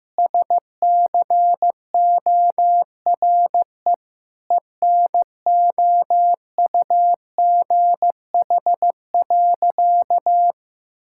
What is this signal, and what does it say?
Morse code, heard as a keyed tone, about 16 wpm, 700 Hz. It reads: SCORE ENOUGH.
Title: Sentences from Top 1600 Words 15wpm